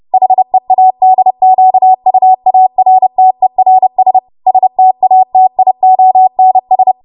Morse code, heard as a keyed tone, about 30 wpm, 750 Hz. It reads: HEADQUARTERS STATIONS
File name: Archive / 30 WPM / 30WPM